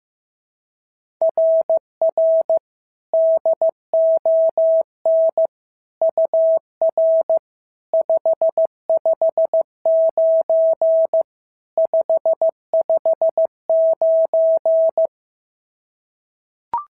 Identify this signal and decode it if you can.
Morse code, heard as a keyed tone, about 14 wpm, 650 Hz. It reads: RR DON UR 559 559
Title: SOTA QSO Practice 15wpm